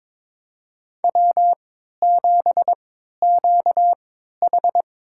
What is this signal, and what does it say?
Morse code, heard as a keyed tone, about 22 wpm, spaced as for 13 wpm, 700 Hz. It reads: W7QH
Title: USA Call Signs - 4 Characters - 3x Character Spacing - 22wpm